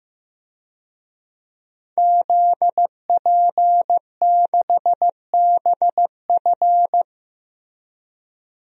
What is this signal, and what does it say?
Morse code, heard as a keyed tone, about 15 wpm, 700 Hz. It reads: ZP6BF
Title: DXCC-100 Call Signs - 5 Characters - Repeated 3x - 15wpm